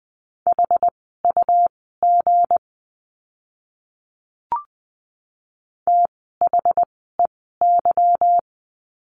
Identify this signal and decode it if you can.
Morse code, heard as a keyed tone, about 20 wpm, spaced as for 14 wpm, 700 Hz. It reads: HUG THEY
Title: 160 Lesson 26f - Letter g - Words 20wpm